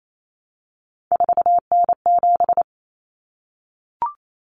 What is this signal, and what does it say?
Morse code, heard as a keyed tone, about 28 wpm, 700 Hz. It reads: 4N7
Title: Common Cut Numbers - 3-Digit 28wpm